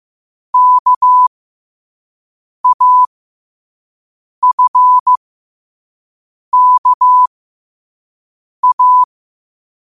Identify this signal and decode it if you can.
Morse code, heard as a keyed tone, about 15 wpm, spaced as for 5 wpm, 1000 Hz. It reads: KAFKA
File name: 5wpm